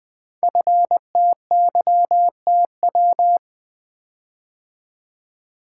Morse code, heard as a keyed tone, about 20 wpm, 700 Hz. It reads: FTYTW
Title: Code Groups - 5 Letters - 20wpm